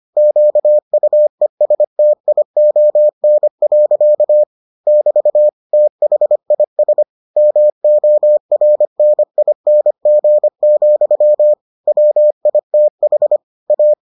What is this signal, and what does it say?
Morse code, heard as a keyed tone, about 25 wpm, 600 Hz. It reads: QUESTION. <BT> THIS MORNING, WITH A